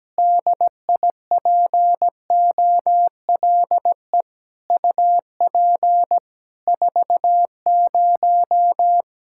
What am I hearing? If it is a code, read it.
Morse code, heard as a keyed tone, about 17 wpm, 700 Hz. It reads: DIPOLE UP 40
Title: K1USN SST QSO Practice 17wpm